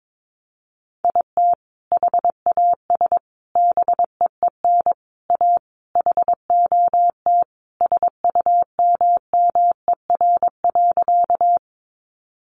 Morse code, heard as a keyed tone, about 22 wpm, 700 Hz. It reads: IT HAS BEEN A HOT SUMMER.
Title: Sentences from Top 1100 Words 22wpm